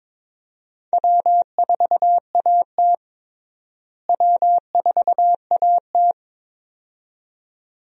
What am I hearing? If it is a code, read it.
Morse code, heard as a keyed tone, about 22 wpm, 700 Hz. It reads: W4AT W4AT
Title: USA Call Signs - 4 Characters - Repeated 2x - 22wpm